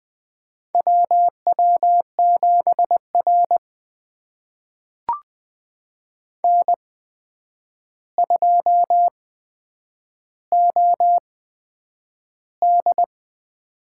Morse code, heard as a keyed tone, about 20 wpm, spaced as for 9 wpm, 700 Hz. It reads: WW7R N2OD
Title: USA Call Signs - 4 Characters - 8x Character Spacing - 20wpm